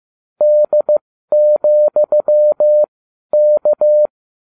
Morse code, heard as a keyed tone, about 15 wpm, spaced as for 12 wpm, 600 Hz. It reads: D,K